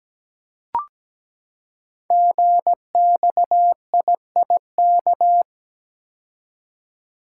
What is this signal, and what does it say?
Morse code, heard as a keyed tone, about 17 wpm, 700 Hz. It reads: GXIIK
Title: Code Groups - 5 Letters - 17wpm